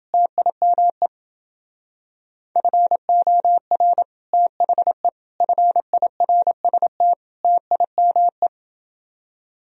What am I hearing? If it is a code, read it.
Morse code, heard as a keyed tone, about 29 wpm, 700 Hz. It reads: TIME FOR THE FIRST TIME